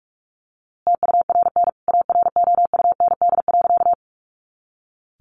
Morse code, heard as a keyed tone, about 44 wpm, 700 Hz. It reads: TURN AROUND.
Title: A Sound Of Thunder 45wpm